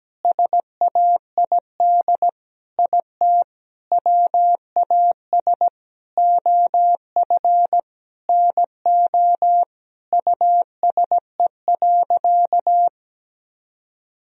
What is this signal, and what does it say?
Morse code, heard as a keyed tone, about 17 wpm, 700 Hz. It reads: SAID IT WAS OF NO USE.
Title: Sentences from Top 1000 Words - Review Part 4 - 17wpm